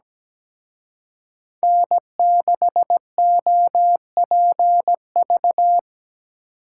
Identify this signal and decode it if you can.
Morse code, heard as a keyed tone, about 20 wpm, 700 Hz. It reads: N6OPV
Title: Speed Racing - USA 1x3 Call Signs 17wpm